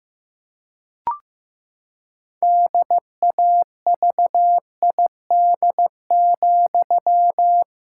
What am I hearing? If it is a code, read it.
Morse code, heard as a keyed tone, about 15 wpm, 700 Hz. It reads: DAVID,
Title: Pangram Challenge v2 - 15wpm